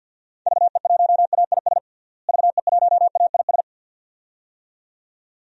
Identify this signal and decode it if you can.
Morse code, heard as a keyed tone, about 49 wpm, 700 Hz. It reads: VE1AIS VE1AIS
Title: DXCC-100 Call Signs - 6 Characters - Repeated 2x - 50wpm